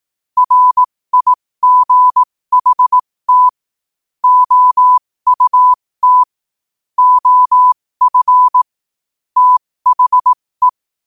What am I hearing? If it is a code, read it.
Morse code, heard as a keyed tone, about 18 wpm, spaced as for 15 wpm, 1000 Hz. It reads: RIGHT OUT OF THE
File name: qotdfiles / 15wpm